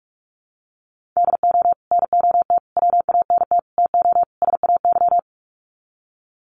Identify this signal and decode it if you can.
Morse code, heard as a keyed tone, about 44 wpm, 700 Hz. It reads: DO NOT WANT TO SAY
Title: Sets of 5 Words Encore 45wpm